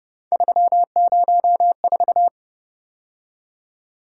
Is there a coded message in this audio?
Yes: Morse code from 0.3 to 2.3 s, heard as a keyed tone, about 29 wpm, 700 Hz. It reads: 304